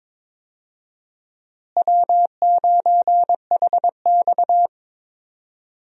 Morse code, heard as a keyed tone, about 22 wpm, 700 Hz. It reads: W9HX
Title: USA Call Signs - 4 Characters - Repeated 2x - 22wpm